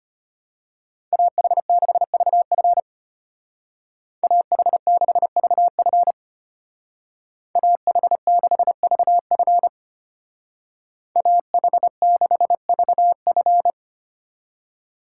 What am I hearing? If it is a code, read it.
Morse code, heard as a keyed tone, about 32 wpm, 700 Hz. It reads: AH6VF AH6VF AH6VF AH6VF